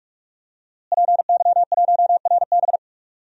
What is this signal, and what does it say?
Morse code, heard as a keyed tone, about 41 wpm, 700 Hz. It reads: PY1RB
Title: DXCC-100 Call Signs - 5 Characters - Repeated 2x - 45wpm